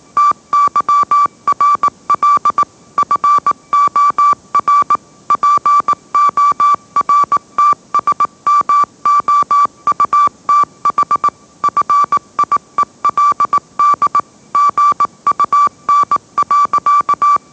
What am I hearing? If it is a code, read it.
Morse code, heard as a keyed tone, about 18 wpm, 1160 Hz. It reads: TYRL FOR PORTSMOUTH FIELD GUN.
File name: mrx / mrx082 20wpm